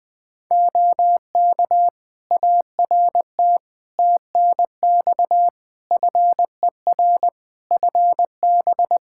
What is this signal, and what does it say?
Morse code, heard as a keyed tone, about 20 wpm, 700 Hz. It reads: OK ART TNX FER FB
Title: Practice QSO - Exchange 4 - Ending - 20wpm